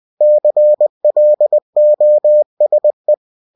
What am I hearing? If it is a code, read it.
Morse code, heard as a keyed tone, about 20 wpm, 600 Hz. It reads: CLOSE